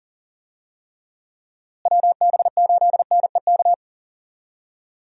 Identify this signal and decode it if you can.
Morse code, heard as a keyed tone, about 40 wpm, 700 Hz. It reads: WB8NEK
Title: USA Call Signs - 6 Characters - Repeated 2x - 40wpm